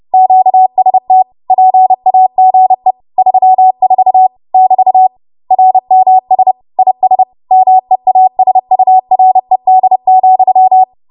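Morse code, heard as a keyed tone, about 30 wpm, 750 Hz. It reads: QST PAGE 34 <BT> RMS IS MEASURED,
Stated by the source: W1AW Web Code Practice Files, 30 WPM run from ARRL; text from April 2023 QST.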